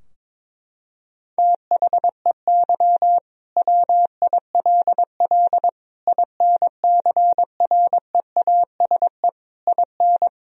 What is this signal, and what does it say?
Morse code, heard as a keyed tone, about 21 wpm, 700 Hz. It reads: THEY WILL INCREASE IN
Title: Sentences from Top 1000 Words - Review Part 4 - 22wpm